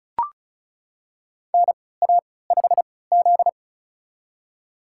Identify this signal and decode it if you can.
Morse code, heard as a keyed tone, about 36 wpm, spaced as for 20 wpm, 700 Hz. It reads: NA5Z
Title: USA Call Signs - 3x Character Spacing - 35wpm